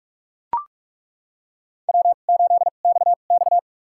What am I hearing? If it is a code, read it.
Morse code, heard as a keyed tone, about 41 wpm, spaced as for 31 wpm, 700 Hz. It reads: W8XX
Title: USA Call Signs - 4 Characters - 2x Character Spacing - 45wpm